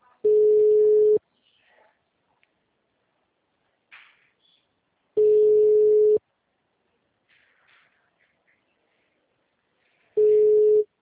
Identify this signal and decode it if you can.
Ringback tone (Europe, 425 Hz).
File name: out- -2014- - - .0